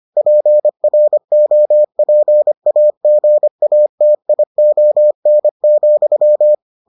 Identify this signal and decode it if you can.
Morse code, heard as a keyed tone, about 25 wpm, 600 Hz. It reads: PROPAGATION,